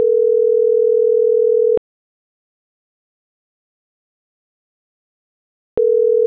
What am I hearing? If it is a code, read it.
Ringback tone (North America), 440 + 480 Hz.